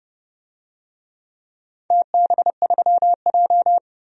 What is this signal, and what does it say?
Morse code, heard as a keyed tone, about 29 wpm, 700 Hz. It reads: TB3J
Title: DXCC-100 Call Signs - 4 Characters - 30wpm